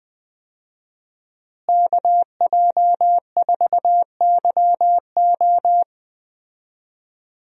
Morse code, heard as a keyed tone, about 20 wpm, 700 Hz. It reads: KJ4YO
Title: USA Call Signs - 5 Characters - Repeated 2x - 20wpm